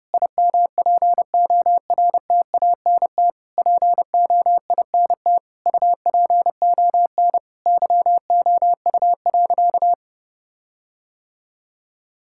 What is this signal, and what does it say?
Morse code, heard as a keyed tone, about 30 wpm, 700 Hz. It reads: IMPORTANT POINT UPON YOU.
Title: Sentences from Top 1600 Words 30wpm